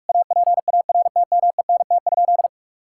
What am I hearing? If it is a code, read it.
Morse code, heard as a keyed tone, about 46 wpm, 700 Hz. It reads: APARTMENT?
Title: Sentences from Top 1000 Words - Review Part 6 - 45wpm